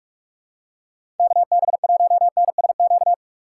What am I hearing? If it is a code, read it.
Morse code, heard as a keyed tone, about 46 wpm, 700 Hz. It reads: KB1NSQ
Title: USA Call Signs - 4x Character Spacing - 45wpm